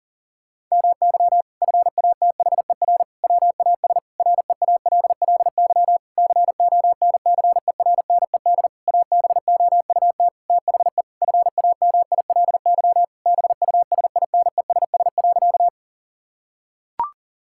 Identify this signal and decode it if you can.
Morse code, heard as a keyed tone, about 40 wpm, 700 Hz. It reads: MY FATHER WAS REALLY CONCERNED ABOUT THE FAMILY BUSINESS.
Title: Sentences from Top 1000 Words - Review Part 6 - 40wpm